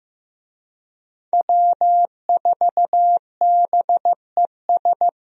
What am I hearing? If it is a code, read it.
Morse code, heard as a keyed tone, about 15 wpm, 700 Hz. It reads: W4BES
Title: USA Call Signs - 5 Characters - Repeated 2x - 15wpm